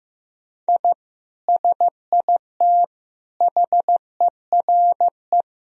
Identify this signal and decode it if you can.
Morse code, heard as a keyed tone, about 16 wpm, 700 Hz. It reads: I SIT HERE
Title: Sentences from Top 1000 Words - Review Part 1 - 15wpm